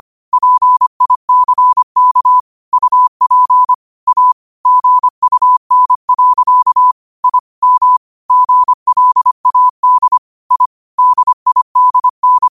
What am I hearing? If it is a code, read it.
Morse code, heard as a keyed tone, about 25 wpm, 1000 Hz. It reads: PICK UP A GUN. I M GLAD I DIDN
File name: qotdfiles / 25wpm